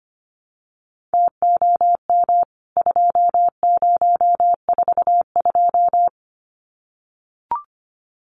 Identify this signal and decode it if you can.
Morse code, heard as a keyed tone, about 25 wpm, 700 Hz. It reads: TOM 2042
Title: CWT Exchange 25wpm